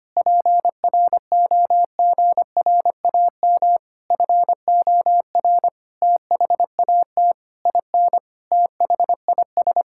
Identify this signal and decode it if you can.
Morse code, heard as a keyed tone, about 25 wpm, 700 Hz. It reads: PROGRAM FOR THAT IN THIS
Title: Sentences from Top 1000 Words - Review Part 5 - 25wpm